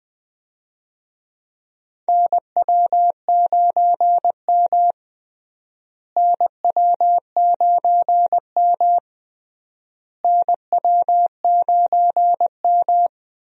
Morse code, heard as a keyed tone, about 20 wpm, 700 Hz. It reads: NW9M NW9M NW9M
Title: USA Call Signs - 4 Characters - Repeated 3x - 20wpm